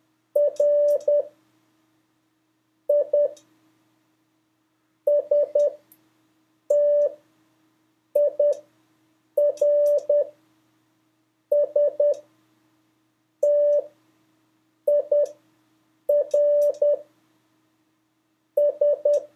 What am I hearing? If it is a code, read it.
Morse code, heard as a keyed tone, about 10 wpm, 580 Hz. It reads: R I S T I R S T I R S
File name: 3of5-words-2